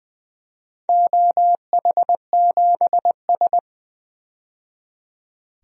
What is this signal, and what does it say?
Morse code, heard as a keyed tone, about 20 wpm, 700 Hz. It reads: OH7S
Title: DXCC-100 Call Signs - 4 Characters - 20wpm